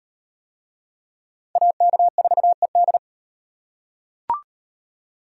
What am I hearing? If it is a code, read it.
Morse code, heard as a keyed tone, about 38 wpm, 700 Hz. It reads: AK4ED